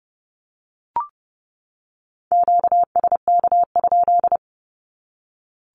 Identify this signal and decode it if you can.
Morse code, heard as a keyed tone, about 30 wpm, 700 Hz. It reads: QSK?
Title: Q-Codes 30wpm